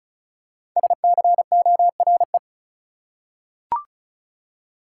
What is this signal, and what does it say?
Morse code, heard as a keyed tone, about 36 wpm, 700 Hz. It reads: SCORE
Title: Five Letter Words - 35wpm